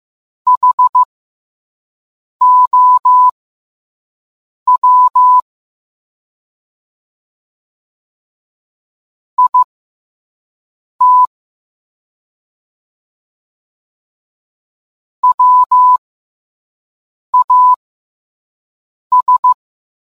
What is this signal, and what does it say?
Morse code, heard as a keyed tone, about 15 wpm, spaced as for 5 wpm, 1000 Hz. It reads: HOW IT WAS